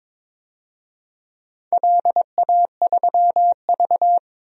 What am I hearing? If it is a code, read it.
Morse code, heard as a keyed tone, about 22 wpm, 700 Hz. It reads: LA3V
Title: DXCC-100 Call Signs - 4 Characters - 22wpm